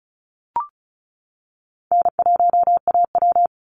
Morse code, heard as a keyed tone, about 34 wpm, 700 Hz. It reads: N1AW
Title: NA SOTA Chaser Callsigns 35wpm